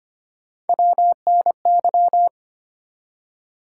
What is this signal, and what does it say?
Morse code, heard as a keyed tone, about 25 wpm, 700 Hz. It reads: WNY
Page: morse-code-ninja-practice-25wpm